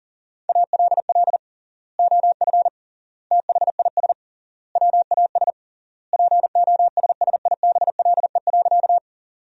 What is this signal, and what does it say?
Morse code, heard as a keyed tone, about 40 wpm, 700 Hz. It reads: ALL OF THIS WAS POSSIBLE.
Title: 3x Word-Spacing - Sentences from Top 500 Words - 40wpm